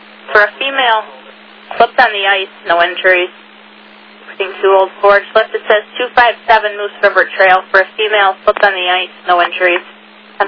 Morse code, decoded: none

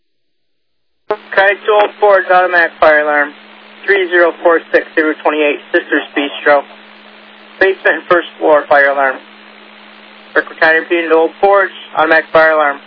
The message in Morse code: none

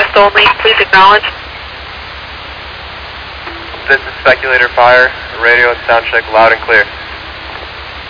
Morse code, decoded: none

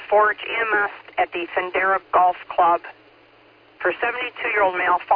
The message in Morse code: none

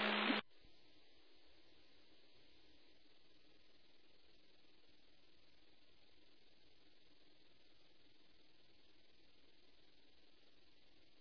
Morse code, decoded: none

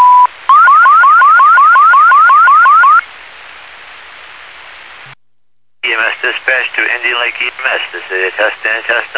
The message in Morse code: none

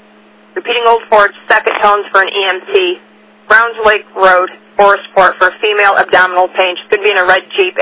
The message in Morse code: none

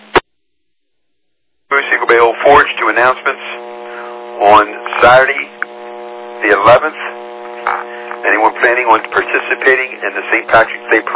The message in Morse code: none

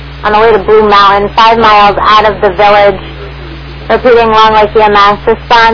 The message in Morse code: none

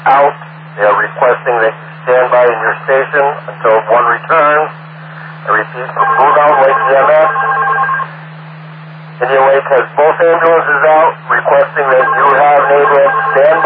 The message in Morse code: none

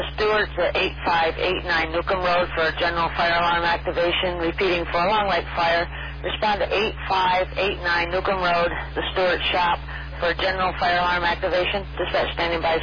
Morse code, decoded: none

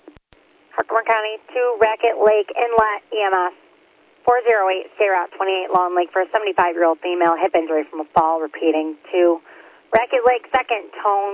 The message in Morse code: none

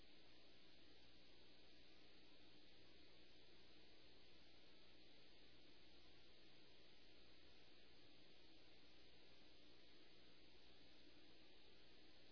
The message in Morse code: none

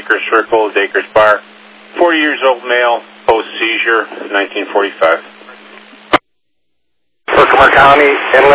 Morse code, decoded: none

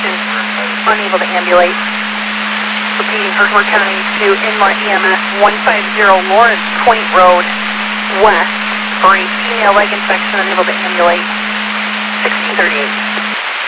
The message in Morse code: none

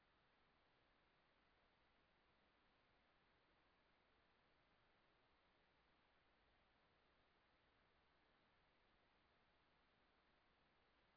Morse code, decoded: none

